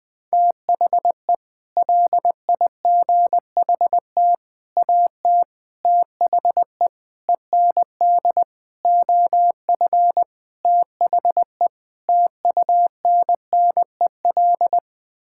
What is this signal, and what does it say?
Morse code, heard as a keyed tone, about 19 wpm, 700 Hz. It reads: THE LIGHT AT THE END OF THE TUNNEL
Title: Sets of 8 Words - 20wpm